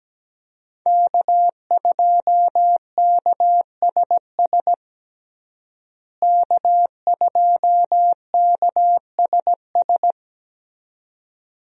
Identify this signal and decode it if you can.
Morse code, heard as a keyed tone, about 17 wpm, 700 Hz. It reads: K2KSS K2KSS
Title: USA Call Signs - 5 Characters - Repeated 2x - 17wpm